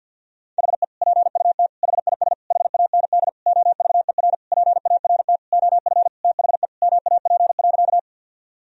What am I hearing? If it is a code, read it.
Morse code, heard as a keyed tone, about 49 wpm, 700 Hz. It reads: HE PUT HIS HAND OVER PART OF THE MAP.
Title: Sentences from Top 1000 Words - Review Part 2 - 50wpm